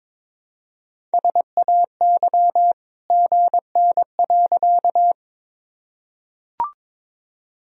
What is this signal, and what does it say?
Morse code, heard as a keyed tone, about 21 wpm, 700 Hz. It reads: SAY GN.
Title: Sentences from Top 300 Words and Abbreviations 22wpm